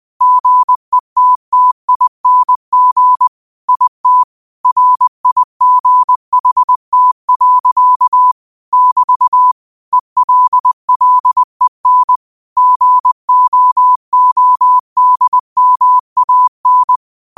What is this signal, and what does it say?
Morse code, heard as a keyed tone, about 20 wpm, 1000 Hz. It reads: GETTING IT RIGHT. <BT> ELLEN GOODMAN
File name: qotdfiles / 20wpm